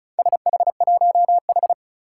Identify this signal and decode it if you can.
Morse code, heard as a keyed tone, about 36 wpm, 700 Hz. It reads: SH1H
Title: Four Challenging Letter-Number Combinations 35wpm